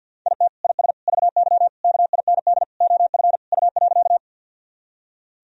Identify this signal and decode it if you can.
Morse code, heard as a keyed tone, about 50 wpm, 700 Hz. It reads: IT IS VY KIND OF U.
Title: Sentences from Top 200 Words and Abbreviations 50wpm